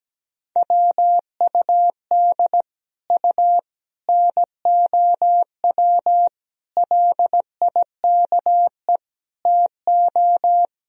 Morse code, heard as a keyed tone, about 17 wpm, 700 Hz. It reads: WUD U NOW LIKE TO
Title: Sentences from Top 200 Words and Abbreviations 17wpm